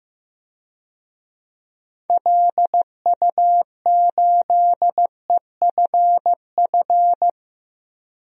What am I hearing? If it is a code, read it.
Morse code, heard as a keyed tone, about 15 wpm, 700 Hz. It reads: LU8EFF
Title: DXCC-100 Call Signs - 6 Characters - 15wpm